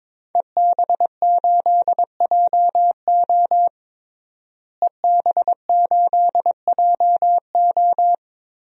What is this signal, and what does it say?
Morse code, heard as a keyed tone, about 22 wpm, 700 Hz. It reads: EB8JO EB8JO
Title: DXCC-100 Call Signs - 5 Characters - Repeated 2x - 22wpm